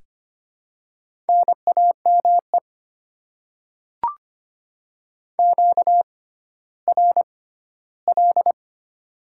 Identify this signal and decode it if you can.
Morse code, heard as a keyed tone, about 25 wpm, spaced as for 14 wpm, 700 Hz. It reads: NAME QRL
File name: Top 20 QSO Elements - 6x Character Spacing - 25wpm